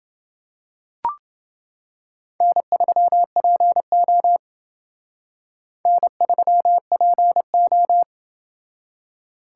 Morse code, heard as a keyed tone, about 30 wpm, 700 Hz. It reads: N3PO N3PO